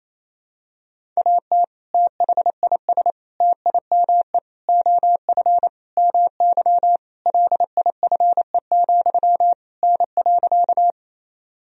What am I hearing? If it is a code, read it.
Morse code, heard as a keyed tone, about 28 wpm, 700 Hz. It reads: AT THIS TIME OF MY LIFE, N.
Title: Sentences from Top 300 Words and Abbreviations 28wpm